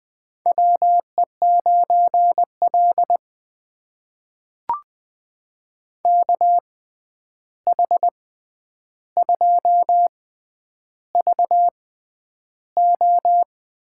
Morse code, heard as a keyed tone, about 21 wpm, spaced as for 10 wpm, 700 Hz. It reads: WE9L KH2VO
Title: USA Call Signs - 6x Character Spacing - 20wpm